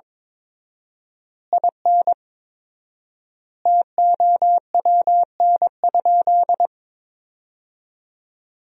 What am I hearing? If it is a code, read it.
Morse code, heard as a keyed tone, about 22 wpm, 700 Hz. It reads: IN TOWN?
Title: 4x Word-Spacing - Sentences from Top 500 Words - 22wpm